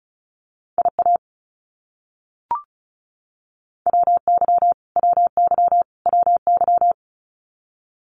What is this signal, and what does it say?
Morse code, heard as a keyed tone, about 34 wpm, 700 Hz. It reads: IA WY WY WY
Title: US States Repeated 3x 35wpm